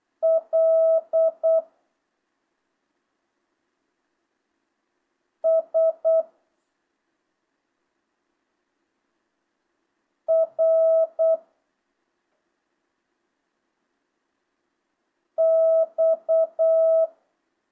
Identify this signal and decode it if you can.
Morse code, heard as a keyed tone, about 8 wpm, 650 Hz. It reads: L S R X
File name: jh8cbh / 2025cw